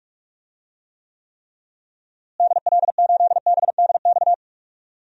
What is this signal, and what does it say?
Morse code, heard as a keyed tone, about 41 wpm, 700 Hz. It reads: DL8BDX
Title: DXCC-100 Call Signs - 6 Characters - Repeated 3x - 45wpm